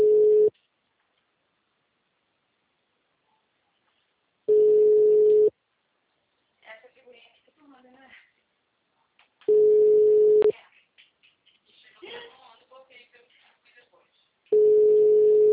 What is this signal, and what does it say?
Ringback tone (Europe, 425 Hz), with speech.